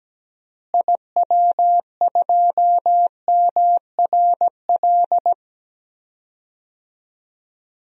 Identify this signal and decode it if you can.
Morse code, heard as a keyed tone, about 17 wpm, 700 Hz. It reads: IW2MRL
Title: DXCC-100 Call Signs - 6 Characters - Repeated 3x - 17wpm